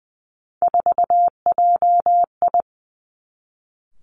Morse code, heard as a keyed tone, about 20 wpm, 700 Hz. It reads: 4JI
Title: Three Challenging Letter-Number Combinations 20wpm